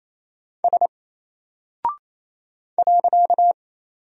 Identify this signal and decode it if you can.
Morse code, heard as a keyed tone, about 28 wpm, 700 Hz. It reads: S .
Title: Single Character (Letter:Number:Punctuation:Prosign) Alt 28wpm